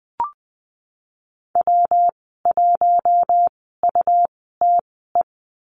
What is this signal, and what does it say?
Morse code, heard as a keyed tone, about 20 wpm, spaced as for 15 wpm, 700 Hz. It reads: W1UTE
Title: 140 Lesson 23g - Letter f - Call Signs 20wpm